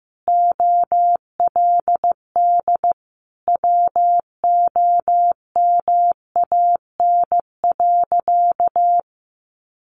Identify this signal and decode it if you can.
Morse code, heard as a keyed tone, about 15 wpm, 700 Hz. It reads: OLD WOMAN.
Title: Sentences from Top 1100 Words 15wpm